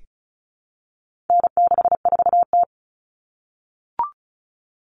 Morse code, heard as a keyed tone, about 34 wpm, 700 Hz. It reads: N64T
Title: Common Cut Numbers - 4-Digit 35wpm